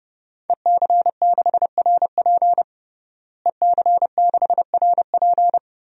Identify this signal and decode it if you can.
Morse code, heard as a keyed tone, about 30 wpm, 700 Hz. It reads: EC6RP EC6RP
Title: DXCC-100 Call Signs - 5 Characters - Repeated 3x - 30wpm